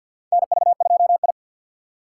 Morse code, heard as a keyed tone, about 47 wpm, 700 Hz. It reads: NV2I
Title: USA Call Signs - 3x Character Spacing - 50wpm